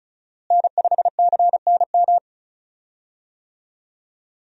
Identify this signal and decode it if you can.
Morse code, heard as a keyed tone, about 35 wpm, 700 Hz. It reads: N5CNM
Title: USA Call Signs - 5 Characters - Repeated 2x - 35wpm